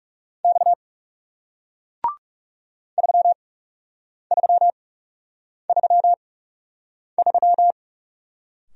Morse code, heard as a keyed tone, about 38 wpm, spaced as for 8 wpm, 700 Hz. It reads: X 3333